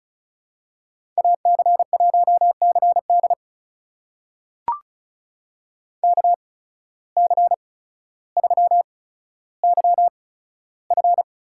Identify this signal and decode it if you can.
Morse code, heard as a keyed tone, about 36 wpm, spaced as for 15 wpm, 700 Hz. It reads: AC1CD KC3YF
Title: USA Call Signs - 5 Characters - 8x Character Spacing - 35wpm